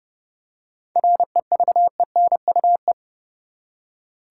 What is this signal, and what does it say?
Morse code, heard as a keyed tone, about 30 wpm, 700 Hz. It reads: REVENUE
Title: Seven Letter Words - 30wpm